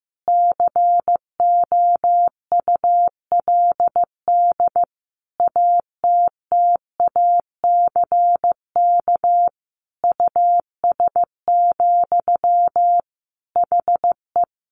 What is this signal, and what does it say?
Morse code, heard as a keyed tone, about 15 wpm, 700 Hz. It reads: COULD ATTACK US, HE
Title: Sentences from Top 1500 Words 15wpm